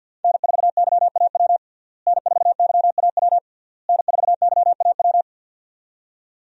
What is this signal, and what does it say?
Morse code, heard as a keyed tone, about 49 wpm, 700 Hz. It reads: N4YAW N4YAW N4YAW
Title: USA Call Signs - 5 Characters - Repeated 3x - 50wpm